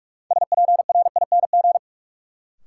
Morse code, heard as a keyed tone, about 44 wpm, 700 Hz. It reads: SPRING
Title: Sentences from Top 1200 Words - New Word Drill Alt - 45wpm